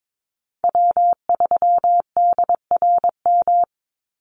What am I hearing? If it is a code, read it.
Morse code, heard as a keyed tone, about 22 wpm, 700 Hz. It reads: W3DRM
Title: USA Call Signs 22wpm